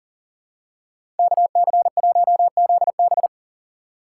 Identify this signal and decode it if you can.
Morse code, heard as a keyed tone, about 40 wpm, 700 Hz. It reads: KC1ZB
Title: USA Call Signs - 5 Characters - 2x Character Spacing - 40wpm